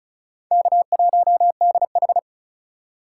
Morse code, heard as a keyed tone, about 36 wpm, 700 Hz. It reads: K1DH
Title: USA Call Signs - 4 Characters - 3x Character Spacing - 35wpm